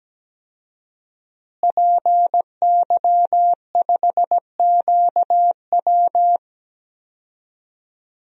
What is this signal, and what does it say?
Morse code, heard as a keyed tone, about 17 wpm, 700 Hz. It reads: PY5QW
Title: DXCC-100 Call Signs - 5 Characters - Repeated 3x - 17wpm